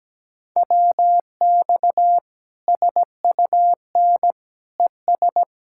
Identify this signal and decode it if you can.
Morse code, heard as a keyed tone, about 17 wpm, 700 Hz. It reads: WX SUN ES
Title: Practice QSO - Exchange 2 - Info - 17wpm